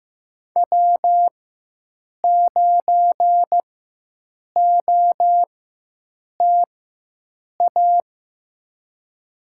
Morse code, heard as a keyed tone, about 15 wpm, spaced as for 7 wpm, 700 Hz. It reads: W9OTA
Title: USA Call Signs - 5 Characters - 4x Character Spacing - 15wpm